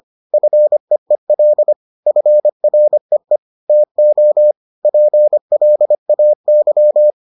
Morse code, heard as a keyed tone, about 25 wpm, 600 Hz. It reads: FEEL FREE TO PLAY